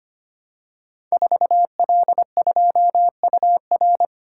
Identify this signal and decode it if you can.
Morse code, heard as a keyed tone, about 25 wpm, 700 Hz. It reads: 4L2UR
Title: Code Groups - 5 Letter:Numbers - 25wpm